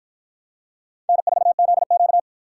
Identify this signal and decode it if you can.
Morse code, heard as a keyed tone, about 53 wpm, 700 Hz. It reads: N4ZQ